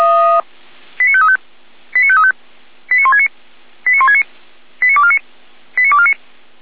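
Five-tone selective call (ZVEI), heard as quick runs of five tones, tones of about 70 ms. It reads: 89535 89535 89169 89169 89249 89249
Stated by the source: ZVEI five-tone selective call received via SDR.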